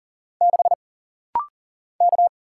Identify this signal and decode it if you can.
Morse code, heard as a keyed tone, about 40 wpm, 700 Hz. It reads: 6 K
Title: Single Letter-Number Alt - Rapid-Fire 40wpm